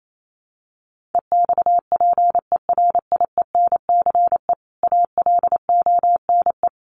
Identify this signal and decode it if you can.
Morse code, heard as a keyed tone, about 28 wpm, 700 Hz. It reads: EXPERIENCE ALONE
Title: Sentences from Top 1000 Words 28wpm